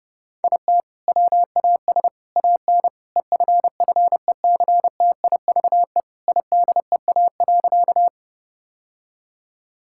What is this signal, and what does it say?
Morse code, heard as a keyed tone, about 30 wpm, 700 Hz. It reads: IT WAS AN EFFECTIVE IDEA.
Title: Sentences from Top 1000 Words - Review Part 6 - 30wpm